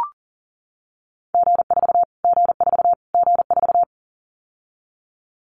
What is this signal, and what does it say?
Morse code, heard as a keyed tone, about 40 wpm, 700 Hz. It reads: G4 G4 G4
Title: Callsign Prefixes for Top 25 DXCC - Repeated 3x - 40wpm